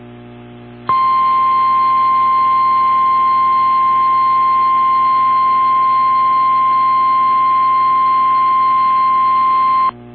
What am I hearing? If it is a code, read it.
Emergency Alert System (SAME) tones (the 1050 Hz attention tone). It is heard from 0.9 to 9.9 s.
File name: eas / KWBE / KWBE